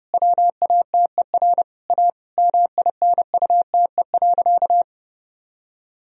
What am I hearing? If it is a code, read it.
Morse code, heard as a keyed tone, about 30 wpm, 700 Hz. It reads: WATER A MINUTE.
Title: Sentences from Top 1000 Words - Review Part 1 - 30wpm